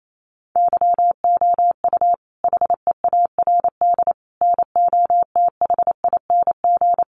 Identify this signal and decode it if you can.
Morse code, heard as a keyed tone, about 28 wpm, 700 Hz. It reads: YOU HEARD NOTHING
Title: Sentences from Top 600 Words 28wpm